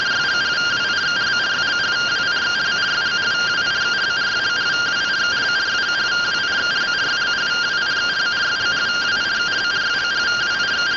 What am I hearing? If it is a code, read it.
Two-tone teleprinter signal, about 170 Hz shift.